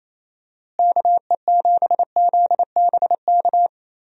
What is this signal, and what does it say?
Morse code, heard as a keyed tone, about 28 wpm, 700 Hz. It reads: KE7ZBK
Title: USA Call Signs - 6 Characters - Repeated 3x - 28wpm